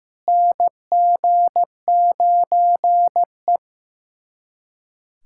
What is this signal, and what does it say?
Morse code, heard as a keyed tone, about 16 wpm, 700 Hz. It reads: NG9E
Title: USA Callsign DE Callsign - 4 Characters - 15wpm